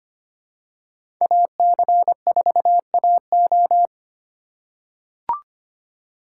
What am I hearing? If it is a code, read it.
Morse code, heard as a keyed tone, about 25 wpm, 700 Hz. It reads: AC4AO
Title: USA Call Signs - 5 Characters - 3x Character Spacing - 25wpm